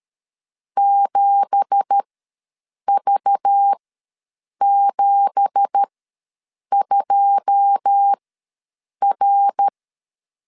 Morse code, heard as a keyed tone, about 13 wpm, spaced as for 7 wpm, 800 Hz. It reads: 7V72R
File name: LEZIONI / 9 2867 ws 12wpm 3431